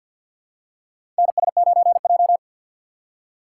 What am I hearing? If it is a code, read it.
Morse code, heard as a keyed tone, about 51 wpm, 700 Hz. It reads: NS9J
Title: USA Call Signs - 6x Character Spacing - 50wpm